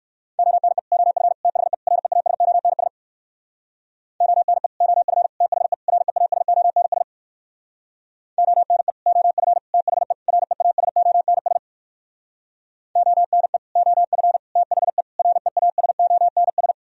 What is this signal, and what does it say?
Morse code, heard as a keyed tone, about 63 wpm, 700 Hz. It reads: ONE OF THE REASONS ONE OF THE REASONS ONE OF THE REASONS ONE OF THE REASONS